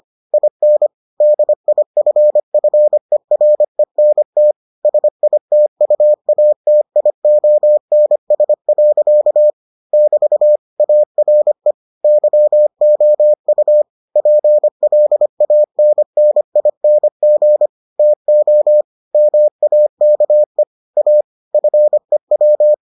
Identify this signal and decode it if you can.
Morse code, heard as a keyed tone, about 25 wpm, 600 Hz. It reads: IN DIFFERENT SITUATIONS. <BT> ARE YOU PLANNING TO MAKE A FEW